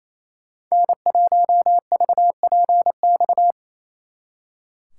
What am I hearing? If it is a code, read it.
Morse code, heard as a keyed tone, about 36 wpm, 700 Hz. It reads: N1VPX